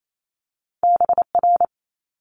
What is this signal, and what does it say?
Morse code, heard as a keyed tone, about 28 wpm, 700 Hz. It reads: BR
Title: Multi-letter Phonemes Quick Fire 28wpm